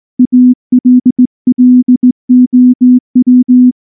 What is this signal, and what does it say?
Morse code, heard as a keyed tone, about 19 wpm, 250 Hz. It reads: ALLOW
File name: morse / morse-training-20WPM-9